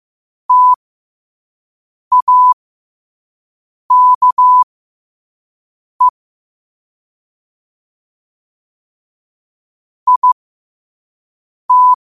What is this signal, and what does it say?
Morse code, heard as a keyed tone, about 15 wpm, spaced as for 5 wpm, 1000 Hz. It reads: TAKE IT
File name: qotdfiles / 5wpm